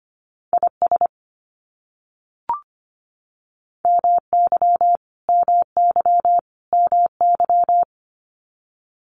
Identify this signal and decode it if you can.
Morse code, heard as a keyed tone, about 25 wpm, 700 Hz. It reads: IS MY MY MY
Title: Top 500 Words Repeated 25wpm